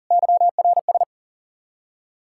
Morse code, heard as a keyed tone, about 50 wpm, 700 Hz. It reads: YRS
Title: Speed-Racing Common QSO Phrases - 40wpm